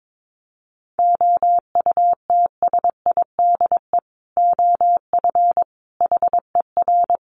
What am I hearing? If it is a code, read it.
Morse code, heard as a keyed tone, about 22 wpm, 700 Hz. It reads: OUTSIDE OF HER
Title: Sentences from Top 1200 Words 22wpm